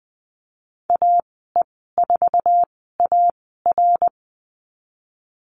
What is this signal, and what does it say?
Morse code, heard as a keyed tone, about 20 wpm, spaced as for 15 wpm, 700 Hz. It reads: AE4AR
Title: 122 Lesson 20g - Number 3 - Call Signs 20wpm